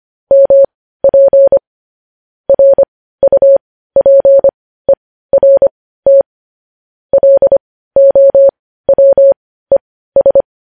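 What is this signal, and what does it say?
Morse code, heard as a keyed tone, about 25 wpm, spaced as for 15 wpm, 550 Hz. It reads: MP RUPERT LOWES
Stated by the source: News headlines in Morse code at 15 words per minute, characters sent at 25 WPM.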